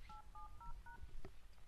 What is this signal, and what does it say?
DTMF, heard as telephone touch tones, tones of about 140 ms. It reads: 870#